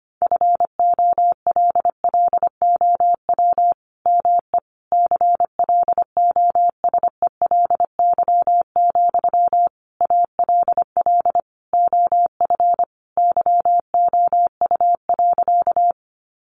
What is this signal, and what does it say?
Morse code, heard as a keyed tone, about 25 wpm, 700 Hz. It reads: FOLLOW ME CLOSELY, ALL OF YOU.
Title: Sentences from Top 1400 Words 25wpm